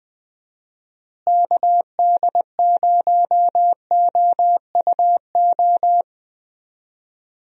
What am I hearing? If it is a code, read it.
Morse code, heard as a keyed tone, about 20 wpm, 700 Hz. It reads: KD0OUO
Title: USA Call Signs - 6 Characters - Repeated 3x - 20wpm